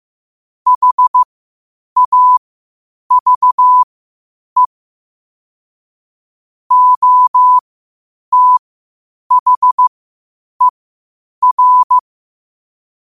Morse code, heard as a keyed tone, about 15 wpm, spaced as for 8 wpm, 1000 Hz. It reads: HAVE OTHER
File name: qotdfiles / 8wpm